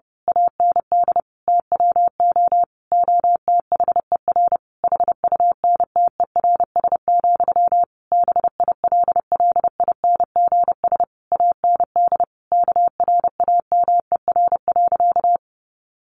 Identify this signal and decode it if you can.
Morse code, heard as a keyed tone, about 30 wpm, 700 Hz. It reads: AND TWO OTHER HUNTERS, BILLINGS AND KRAMER.
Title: A Sound Of Thunder 30wpm